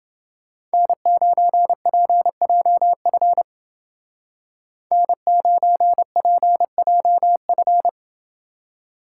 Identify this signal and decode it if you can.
Morse code, heard as a keyed tone, about 29 wpm, 700 Hz. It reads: N9PJF N9PJF